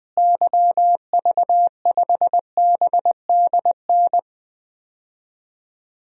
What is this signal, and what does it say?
Morse code, heard as a keyed tone, about 20 wpm, 700 Hz. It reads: YV5BDN
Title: DXCC-100 Call Signs - 6 Characters - Repeated 2x - 20wpm